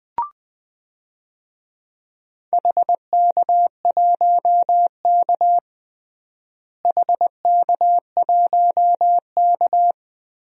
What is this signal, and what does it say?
Morse code, heard as a keyed tone, about 20 wpm, 700 Hz. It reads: HK1K HK1K
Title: DXCC-100 Call Signs - 4 Characters - Repeated 2x - 20wpm